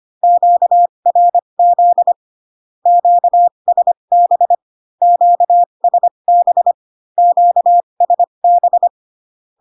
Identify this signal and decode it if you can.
Morse code, heard as a keyed tone, about 25 wpm, spaced as for 22 wpm, 700 Hz. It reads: QRZ QSB QSB QSB
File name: Q-codes most common repeated 3 times 18 25wpm